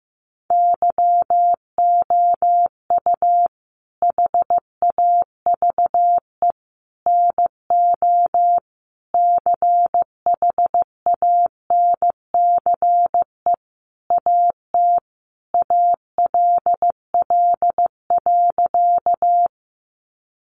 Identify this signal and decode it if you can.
Morse code, heard as a keyed tone, about 15 wpm, 700 Hz. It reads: YOU HAVE NO CHANCE AT ALL.
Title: Sentences from Top 800 Words 15wpm